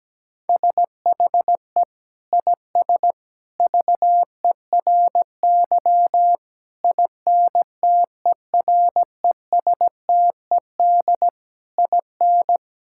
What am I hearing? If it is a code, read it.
Morse code, heard as a keyed tone, about 17 wpm, 700 Hz. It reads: SHE IS VERY INTERESTED IN
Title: Sentences from Top 1000 Words - Review Part 6 - 17wpm